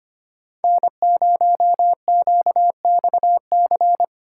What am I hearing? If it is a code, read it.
Morse code, heard as a keyed tone, about 25 wpm, 700 Hz. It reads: N0QXC
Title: USA Call Signs - 5 Characters - 4x Character Spacing - 25wpm